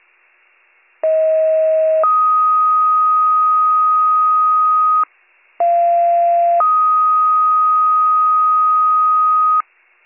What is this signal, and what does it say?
Two-tone page, heard as a one-second tone, then a three-second tone, about 640 then 1250 Hz, about 690 then 1250 Hz. It is heard from 1.0 to 9.6 s.